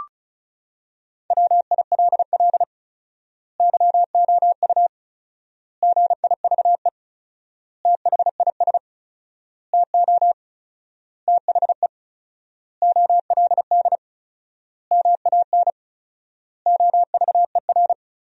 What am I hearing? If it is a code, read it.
Morse code, heard as a keyed tone, about 36 wpm, 700 Hz. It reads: WILL YOU GIVE THIS TO THE OLD MAN OVER
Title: 4x Word-Spacing - Sentences from Top 200 Words - 35wpm